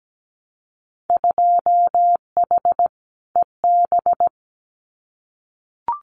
Morse code, heard as a keyed tone, about 17 wpm, 700 Hz. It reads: 2H EB
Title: Winter Field Day Exchanges Encore 17wpm